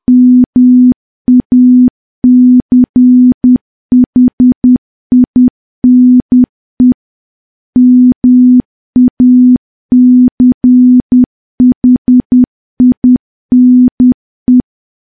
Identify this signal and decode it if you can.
Morse code, heard as a keyed tone, about 10 wpm, 250 Hz. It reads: MACHINE MACHINE
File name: morse / morse-training-10WPM